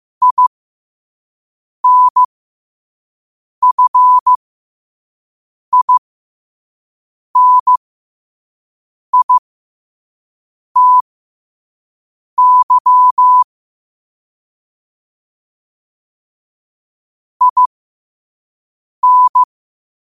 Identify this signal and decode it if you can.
Morse code, heard as a keyed tone, about 15 wpm, spaced as for 5 wpm, 1000 Hz. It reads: INFINITY IN
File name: qotdfiles / 5wpm